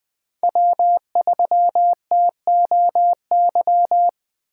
Morse code, heard as a keyed tone, about 20 wpm, 700 Hz. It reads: W3TOY
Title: USA Call Signs - 5 Characters - 4x Character Spacing - 20wpm